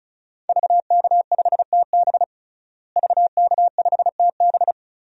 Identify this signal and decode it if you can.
Morse code, heard as a keyed tone, about 36 wpm, 700 Hz. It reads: VK5TB VK5TB
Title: DXCC-100 Call Signs - 5 Characters - Repeated 3x - 35wpm